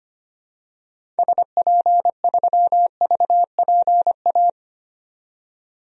Morse code, heard as a keyed tone, about 25 wpm, 700 Hz. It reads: SP3VPA
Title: DXCC-100 Call Signs - 6 Characters - 25wpm